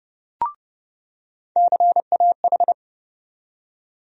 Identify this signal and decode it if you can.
Morse code, heard as a keyed tone, about 31 wpm, 700 Hz. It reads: CAH
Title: USA Call Sign Postfixes - 3 Letters 30wpm